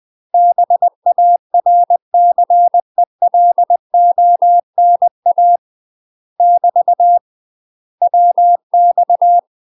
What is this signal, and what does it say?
Morse code, heard as a keyed tone, about 20 wpm, 700 Hz. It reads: BARCELONA <BT> WX
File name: UK MASTER QSOs TRAINING/20WPM